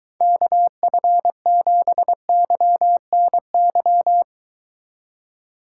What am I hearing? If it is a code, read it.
Morse code, heard as a keyed tone, about 23 wpm, 700 Hz. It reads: KF7YNY